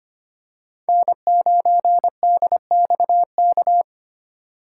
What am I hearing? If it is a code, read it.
Morse code, heard as a keyed tone, about 25 wpm, 700 Hz. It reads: N9DXK
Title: USA Call Signs - 5 Characters - 2x Character Spacing - 25wpm